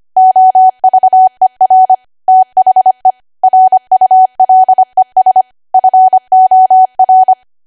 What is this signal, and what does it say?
Morse code, heard as a keyed tone, about 25 wpm, 750 Hz. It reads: OVER THE RULES FOR
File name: Archive / 25 WPM / 25WPM